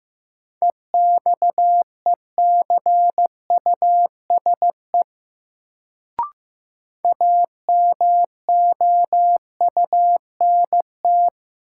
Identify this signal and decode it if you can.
Morse code, heard as a keyed tone, about 15 wpm, 700 Hz. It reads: EXECUSE AMOUNT
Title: Sentences from Top 700 Words - New Word Drill - 15wpm